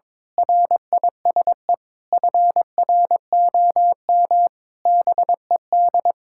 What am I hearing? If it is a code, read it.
Morse code, heard as a keyed tone, about 22 wpm, 700 Hz. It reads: RISE FROM BED
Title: Sentences from Top 1000 Words - Review Part 6 - 22wpm